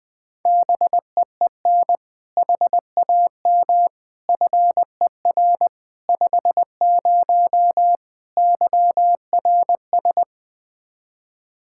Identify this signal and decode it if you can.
Morse code, heard as a keyed tone, about 20 wpm, 700 Hz. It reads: BEEN HAM FER 50 YRS
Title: Practice QSO - Exchange 3 - More Info - 20wpm